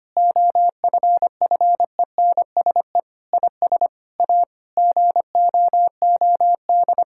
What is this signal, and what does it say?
Morse code, heard as a keyed tone, about 25 wpm, 700 Hz. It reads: OFFENSE IS A GOOD
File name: practice / 25wpm / Difficult Sentences with Hints - 25wpm